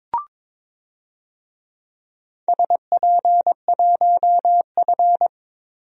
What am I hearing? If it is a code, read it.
Morse code, heard as a keyed tone, about 22 wpm, 700 Hz. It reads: SP1F
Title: DXCC-100 Call Signs - 4 Characters - Repeated 2x - 22wpm